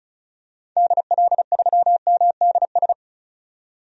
Morse code, heard as a keyed tone, about 36 wpm, 700 Hz. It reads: DL3MDS
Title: DXCC-100 Call Signs - 6 Characters - 35wpm